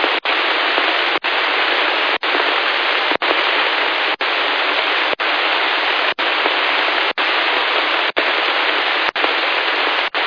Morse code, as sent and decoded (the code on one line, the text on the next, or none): none